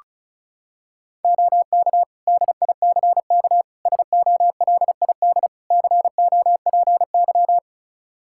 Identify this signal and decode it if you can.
Morse code, heard as a keyed tone, about 36 wpm, 700 Hz. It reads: OK DICK SOLID COPY
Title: Practice QSO - Exchange 3 - More Info - 35wpm